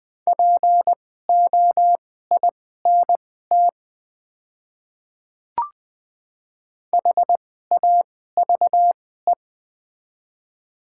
Morse code, heard as a keyed tone, about 20 wpm, spaced as for 15 wpm, 700 Hz. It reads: POINT HAVE
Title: 2x Character Spacing - Top 300 Words - 20wpm